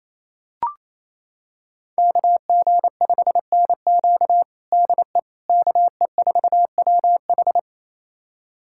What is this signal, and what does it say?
Morse code, heard as a keyed tone, about 27 wpm, 700 Hz. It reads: KG5NQ DE KE4WH
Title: USA Callsign DE Callsign 28wpm